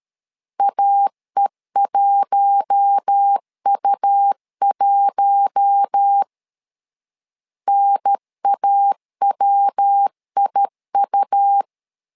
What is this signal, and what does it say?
Morse code, heard as a keyed tone, about 13 wpm, 800 Hz. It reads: AE1U1 NAWIU
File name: LEZIONI / 4 Q1 12wpm 2028